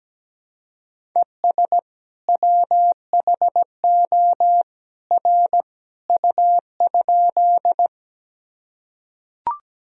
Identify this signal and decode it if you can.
Morse code, heard as a keyed tone, about 17 wpm, 700 Hz. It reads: ES WHO R U?
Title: Sentences from Top 100 Words and Abbreviations 17wpm